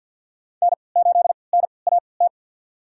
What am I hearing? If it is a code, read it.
Morse code, heard as a keyed tone, about 53 wpm, spaced as for 29 wpm, 700 Hz. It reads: N8NAT
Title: USA Call Signs - 5 Characters - 3x Character Spacing - 50wpm